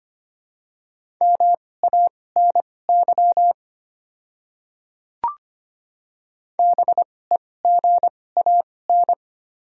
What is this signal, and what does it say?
Morse code, heard as a keyed tone, about 25 wpm, spaced as for 18 wpm, 700 Hz. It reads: MANY BEGAN
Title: 2x Character Spacing - Top 500 Words - 25wpm